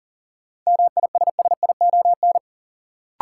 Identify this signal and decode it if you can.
Morse code, heard as a keyed tone, about 40 wpm, 700 Hz. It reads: MISSION
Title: Sentences from Top 1600 Words - New Word Drill Alt - 40wpm